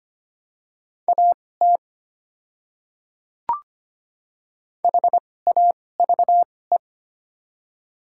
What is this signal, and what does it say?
Morse code, heard as a keyed tone, about 25 wpm, spaced as for 18 wpm, 700 Hz. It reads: AT HAVE
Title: 2x Character Spacing - Top 100 Words - 25wpm